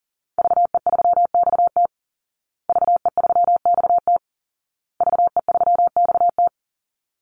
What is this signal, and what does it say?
Morse code, heard as a keyed tone, about 40 wpm, 700 Hz. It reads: VE3XT VE3XT VE3XT
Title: Call Signs Repeated 2x - 40wpm